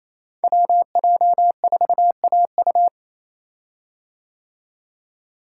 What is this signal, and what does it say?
Morse code, heard as a keyed tone, about 27 wpm, 690 Hz. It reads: WJ4AU
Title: Code Groups - 5 Letter:Numbers - 28wpm